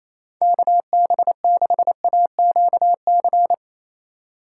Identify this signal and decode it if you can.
Morse code, heard as a keyed tone, about 27 wpm, 700 Hz. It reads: KB6AQC
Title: USA Call Signs - 6 Characters - 2x Character Spacing - 28wpm